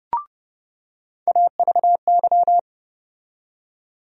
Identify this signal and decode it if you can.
Morse code, heard as a keyed tone, about 31 wpm, 700 Hz. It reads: AVY
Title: USA Call Sign Postfixes - 3 Letters 30wpm